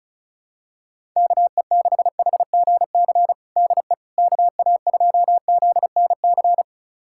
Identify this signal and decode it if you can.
Morse code, heard as a keyed tone, about 34 wpm, 700 Hz. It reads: KE6HGC DE KA2ZNC
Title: USA Callsign DE Callsign - 6 Characters - 35wpm